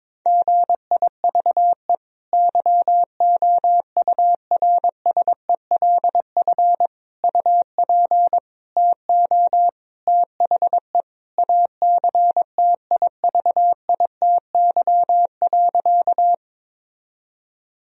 Morse code, heard as a keyed tone, about 22 wpm, 700 Hz. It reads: GIVE YOURSELF UP TO THE ACTIVITY.
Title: Sentences from Top 1000 Words - Review Part 4 - 22wpm